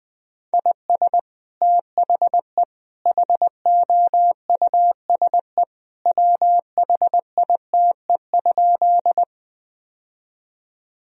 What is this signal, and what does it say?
Morse code, heard as a keyed tone, about 20 wpm, 700 Hz. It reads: IS THE HOUSE WHITE?
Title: Sentences from Top 1000 Words - Review Part 1 - 20wpm